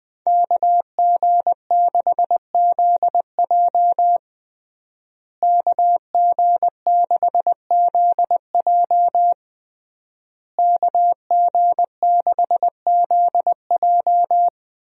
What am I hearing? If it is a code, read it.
Morse code, heard as a keyed tone, about 21 wpm, 700 Hz. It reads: KG6ZJ KG6ZJ KG6ZJ
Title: USA Call Signs - 5 Characters - Repeated 3x - 20wpm